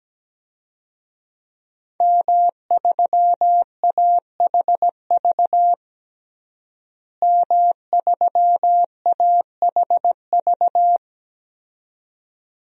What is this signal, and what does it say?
Morse code, heard as a keyed tone, about 17 wpm, 700 Hz. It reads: M3AHV M3AHV
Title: DXCC-100 Call Signs - 5 Characters - Repeated 2x - 17wpm